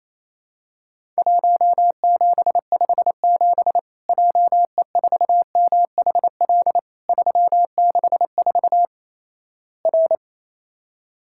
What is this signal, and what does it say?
Morse code, heard as a keyed tone, about 28 wpm, 700 Hz. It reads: 1757 JE4MHL 364 R
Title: Worked All Europe DX Contest - QTC Exchange - 28wpm